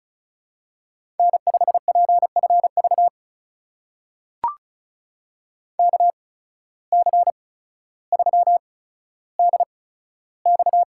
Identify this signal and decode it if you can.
Morse code, heard as a keyed tone, about 36 wpm, spaced as for 15 wpm, 700 Hz. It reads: N5PFV KC3DX
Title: USA Call Signs - 5 Characters - 8x Character Spacing - 35wpm